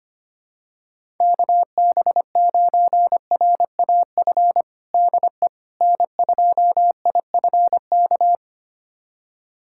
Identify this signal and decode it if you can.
Morse code, heard as a keyed tone, about 25 wpm, 700 Hz. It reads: KB9RAF DE N2IFK
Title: USA Callsign DE Callsign 25wpm